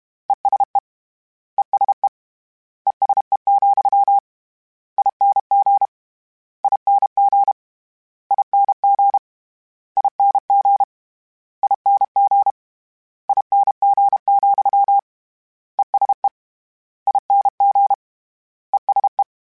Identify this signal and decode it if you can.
Morse code, heard as a keyed tone, about 32 wpm, 800 Hz. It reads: ESE ESE ESE, ING ING ING ING ING ING, ESE ING ESE
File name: General / General-mp3s / ESE ING @30